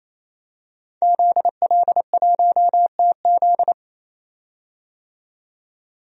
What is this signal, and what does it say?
Morse code, heard as a keyed tone, about 28 wpm, 700 Hz. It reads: ZL1TZ
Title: DXCC-100 Call Signs - 5 Characters - 28wpm